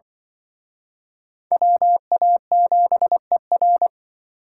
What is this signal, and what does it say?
Morse code, heard as a keyed tone, about 26 wpm, 700 Hz. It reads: WA7ER